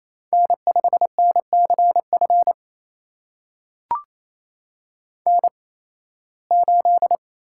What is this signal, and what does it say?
Morse code, heard as a keyed tone, about 28 wpm, spaced as for 18 wpm, 700 Hz. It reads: N5NCF N8
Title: USA Call Signs - 5 Characters - 8x Character Spacing - 28wpm